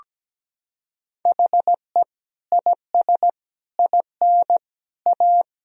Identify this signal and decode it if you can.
Morse code, heard as a keyed tone, about 17 wpm, 700 Hz. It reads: HE IS IN A
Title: Sentences from Top 1000 Words - Review Part 1 - 17wpm